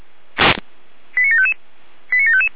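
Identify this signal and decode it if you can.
Five-tone selective call (ZVEI), heard as quick runs of five tones, tones of about 70 ms. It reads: 89755 89755
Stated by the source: ZVEI five-tone selective call 89531, received by SDR.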